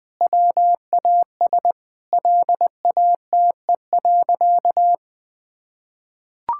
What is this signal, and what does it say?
Morse code, heard as a keyed tone, about 21 wpm, 700 Hz. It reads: WAS LATE.
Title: Sentences with 2-Word Phrases - Spoken Beginning - 20wpm